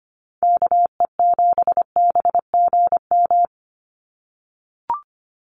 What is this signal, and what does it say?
Morse code, heard as a keyed tone, about 25 wpm, 700 Hz. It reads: KE7BGM
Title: NA SOTA Activator Callsigns 25wpm